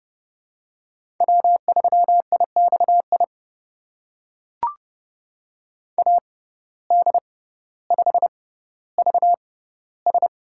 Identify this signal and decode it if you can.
Morse code, heard as a keyed tone, about 31 wpm, spaced as for 16 wpm, 700 Hz. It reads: W3IXI AD5VS
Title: USA Call Signs - 5 Characters - 6x Character Spacing - 30wpm